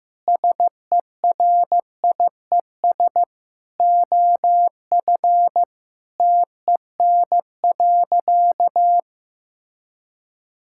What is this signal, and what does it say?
Morse code, heard as a keyed tone, about 15 wpm, 700 Hz. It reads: SERIES OF TEN.
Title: Sentences from Top 1000 Words - Review Part 5 - 15wpm